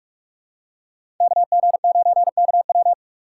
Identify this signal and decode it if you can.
Morse code, heard as a keyed tone, about 44 wpm, 690 Hz. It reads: KG9KW
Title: USA Call Signs - 5 Characters - 8x Character Spacing - 45wpm